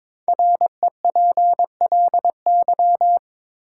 Morse code, heard as a keyed tone, about 22 wpm, 700 Hz. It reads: REPLY
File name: 5 Letter Words - First 3 Letters Given - 22wpm